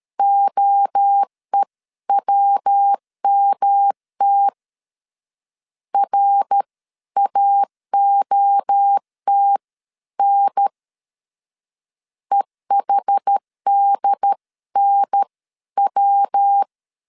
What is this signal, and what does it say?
Morse code, heard as a keyed tone, about 13 wpm, spaced as for 11 wpm, 800 Hz. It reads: OEWMT RAOTN EHDNW
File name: CW / DET / 1 3 1500C S B 12wpm 2828